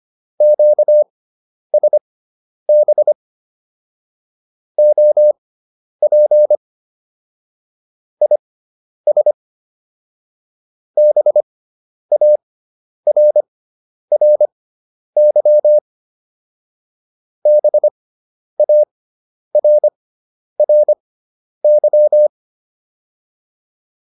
Morse code, heard as a keyed tone, about 25 wpm, spaced as for 10 wpm, 600 Hz. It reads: QSB OP IS BARRY BARRY